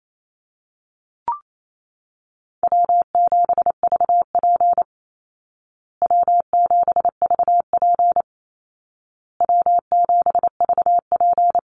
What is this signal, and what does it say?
Morse code, heard as a keyed tone, about 28 wpm, 700 Hz. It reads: W7VP W7VP W7VP
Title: Easy Call Signs Repeated 3x - 28wpm